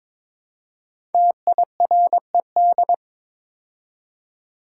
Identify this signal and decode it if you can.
Morse code, heard as a keyed tone, about 22 wpm, 700 Hz. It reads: TIRED
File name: 5 Letter Words - First 3 Letters Given - 22wpm